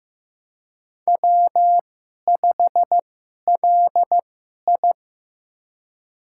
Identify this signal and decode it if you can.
Morse code, heard as a keyed tone, about 15 wpm, spaced as for 11 wpm, 700 Hz. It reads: W5LI
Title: USA Call Signs - 4 Characters - 2x Character Spacing - 15wpm